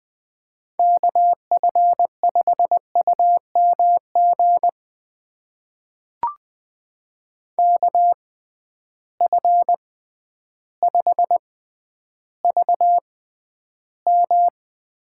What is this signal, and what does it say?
Morse code, heard as a keyed tone, about 20 wpm, spaced as for 11 wpm, 700 Hz. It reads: KF5UMG KF5VM
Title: USA Call Signs - 6 Characters - 6x Character Spacing - 20wpm